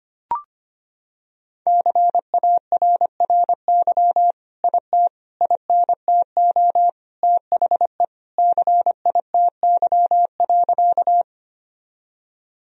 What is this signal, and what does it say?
Morse code, heard as a keyed tone, about 25 wpm, 700 Hz. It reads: CARRY IT INTO THE CITY.
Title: Sentences from Top 1000 Words - Review Part 1 - 25wpm